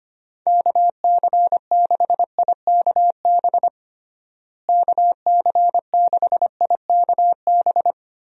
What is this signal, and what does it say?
Morse code, heard as a keyed tone, about 25 wpm, 700 Hz. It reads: KC6IKB KC6IKB
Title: USA Call Signs - 6 Characters - Repeated 3x - 25wpm